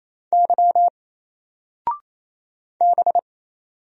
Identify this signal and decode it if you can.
Morse code, heard as a keyed tone, about 27 wpm, 700 Hz. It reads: Y B
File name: Single-Letter-Number-28wpm